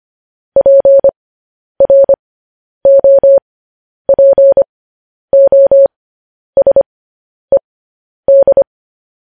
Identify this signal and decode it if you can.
Morse code, heard as a keyed tone, about 25 wpm, spaced as for 10 wpm, 550 Hz. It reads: PROPOSED